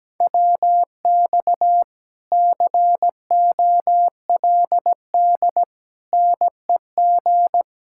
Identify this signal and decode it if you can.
Morse code, heard as a keyed tone, about 18 wpm, 700 Hz. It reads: WX COLD NEG
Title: Speed-Racing Common QSO Phrases - 17wpm